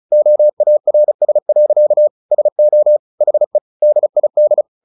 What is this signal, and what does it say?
Morse code, heard as a keyed tone, about 34 wpm, 600 Hz. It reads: OARS. SO HE DID
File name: the old man and the sea 2 35wpm